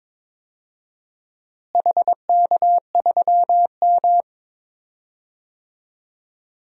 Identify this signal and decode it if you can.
Morse code, heard as a keyed tone, about 22 wpm, 700 Hz. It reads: HK3M
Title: DXCC-100 Call Signs - 4 Characters - Repeated 2x - 22wpm